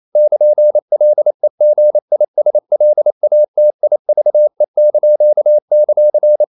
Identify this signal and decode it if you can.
Morse code, heard as a keyed tone, about 28 wpm, 600 Hz. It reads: <KN>LEGISLATIVE);
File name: CW / Practice / NationalCapitals28